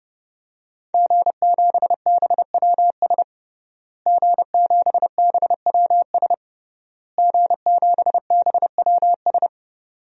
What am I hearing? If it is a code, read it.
Morse code, heard as a keyed tone, about 30 wpm, 700 Hz. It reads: G7BWS G7BWS G7BWS
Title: DXCC-100 Call Signs - 5 Characters - Repeated 3x - 30wpm